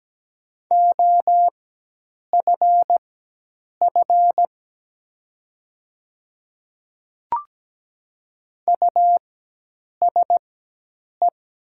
Morse code, heard as a keyed tone, about 17 wpm, spaced as for 8 wpm, 700 Hz. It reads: OFF USE
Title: 4x Character Spacing - Top 200 Words - 17wpm